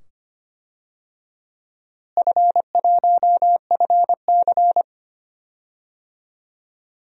Morse code, heard as a keyed tone, about 25 wpm, 700 Hz. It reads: F1FC
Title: DXCC-100 Call Signs - 4 Characters - Repeated 2x - 25wpm